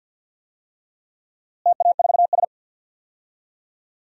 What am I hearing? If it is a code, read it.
Morse code, heard as a keyed tone, about 52 wpm, 700 Hz. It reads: TA4S
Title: DXCC-100 Call Signs - 4 Characters - Repeated 3x - 50wpm